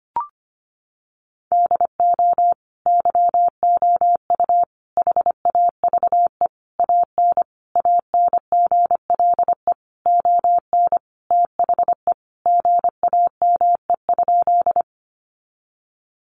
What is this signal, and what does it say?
Morse code, heard as a keyed tone, about 25 wpm, 700 Hz. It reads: DO YOU HAVE AN ANGLE ON THE GAME?
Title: Sentences from Top 1500 Words 25wpm